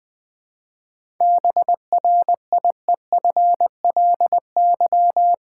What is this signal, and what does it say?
Morse code, heard as a keyed tone, about 20 wpm, 700 Hz. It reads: BRIEFLY
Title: 7 Letter Words - First 3 Letters Given - 20wpm